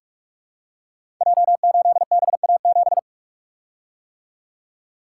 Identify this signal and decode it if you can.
Morse code, heard as a keyed tone, about 44 wpm, 700 Hz. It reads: J8BA7
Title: Code Groups - 5 Letter:Numbers - 45wpm